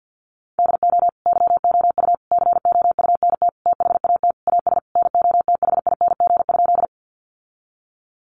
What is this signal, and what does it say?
Morse code, heard as a keyed tone, about 50 wpm, 700 Hz. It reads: DO YOU COUNT THAT AS NOTHING?
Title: Sentences from Top 1500 Words 50wpm